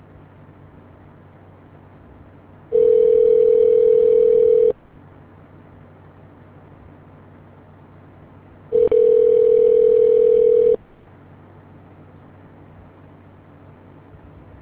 Ringback tone (North America), 440 + 480 Hz.